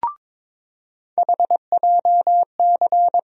Morse code, heard as a keyed tone, about 22 wpm, 700 Hz. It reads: HJC
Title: USA Call Sign Postfixes - 3 Letters 22wpm